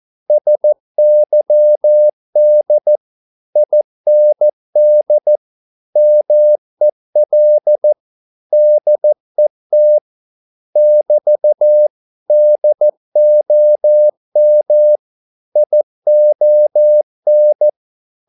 Morse code, heard as a keyed tone, about 14 wpm, 600 Hz. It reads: SYD IND MEL DET <BT> DOM ION
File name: Korte ord DK 14wpm